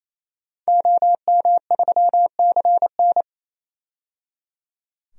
Morse code, heard as a keyed tone, about 28 wpm, 700 Hz. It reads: OM3CN
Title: DXCC-100 Call Signs - 5 Characters - 28wpm